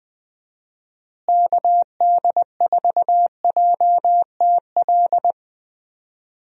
Morse code, heard as a keyed tone, about 20 wpm, 700 Hz. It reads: KD4JTL
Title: USA Call Signs - 6 Characters - Repeated 2x - 20wpm